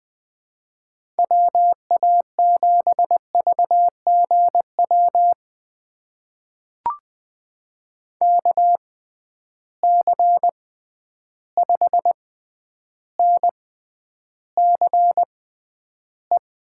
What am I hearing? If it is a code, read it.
Morse code, heard as a keyed tone, about 20 wpm, spaced as for 10 wpm, 700 Hz. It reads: WA7VGW KC5NCE
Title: USA Call Signs - 6 Characters - 6x Character Spacing - 20wpm